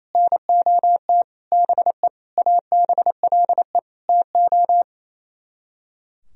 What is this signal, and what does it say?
Morse code, heard as a keyed tone, about 36 wpm, 700 Hz. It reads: NOT BE ABLE TO